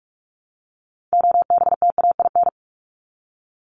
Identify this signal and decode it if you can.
Morse code, heard as a keyed tone, about 44 wpm, 700 Hz. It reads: OBTAIN
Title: ED Suffix - Top 100 Words with Base Words 45wpm